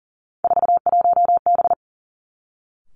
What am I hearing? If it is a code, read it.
Morse code, heard as a keyed tone, about 38 wpm, 700 Hz. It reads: 41B
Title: Three Challenging Letter-Number Combinations 40wpm